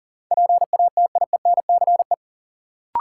Morse code, heard as a keyed tone, about 40 wpm, 700 Hz. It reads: PATIENCE
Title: Sentences from Top 1300 Words - New Word Drill - 40wpm